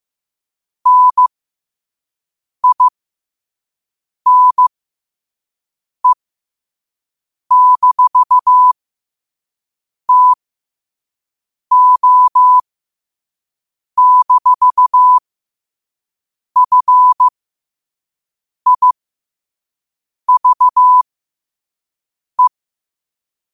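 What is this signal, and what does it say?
Morse code, heard as a keyed tone, about 15 wpm, spaced as for 5 wpm, 1000 Hz. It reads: NINE-TO-FIVE
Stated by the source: 5 WPM Morse quote of the day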